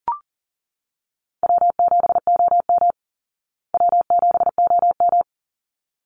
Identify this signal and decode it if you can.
Morse code, heard as a keyed tone, about 38 wpm, 700 Hz. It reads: W7OM W7OM
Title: Call Signs Repeated 3x - 40wpm